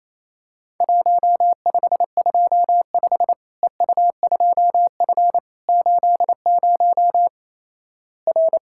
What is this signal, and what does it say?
Morse code, heard as a keyed tone, about 28 wpm, 700 Hz. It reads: 1525 EU2F 80 R
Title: Worked All Europe DX Contest - QTC Exchange - 28wpm